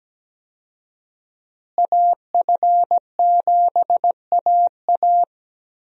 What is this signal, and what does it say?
Morse code, heard as a keyed tone, about 17 wpm, 700 Hz. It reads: AF7AA
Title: USA Call Signs - 5 Characters - Repeated 3x - 17wpm